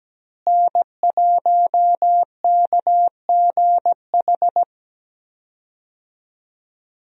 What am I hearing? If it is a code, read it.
Morse code, heard as a keyed tone, about 17 wpm, 700 Hz. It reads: N1KGH
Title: USA Call Signs - 5 Characters - Repeated 2x - 17wpm